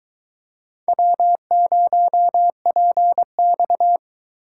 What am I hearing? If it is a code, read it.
Morse code, heard as a keyed tone, about 22 wpm, 700 Hz. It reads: W0PX